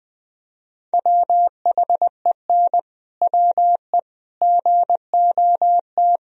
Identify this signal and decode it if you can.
Morse code, heard as a keyed tone, about 21 wpm, 700 Hz. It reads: WHEN WE GOT
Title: Sentences from Top 500 Words and Abbreviations 20wpm